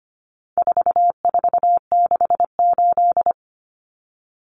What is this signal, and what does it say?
Morse code, heard as a keyed tone, about 25 wpm, 700 Hz. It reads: 4468
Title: Numbers - 4-Digit 25wpm